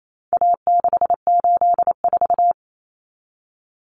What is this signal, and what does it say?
Morse code, heard as a keyed tone, about 28 wpm, 700 Hz. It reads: A684
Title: Common Cut Numbers - 4-Digit 28wpm